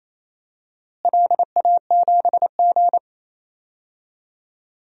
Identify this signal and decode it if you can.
Morse code, heard as a keyed tone, about 28 wpm, 700 Hz. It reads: LA7G
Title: DXCC-100 Call Signs - 4 Characters - Repeated 2x - 28wpm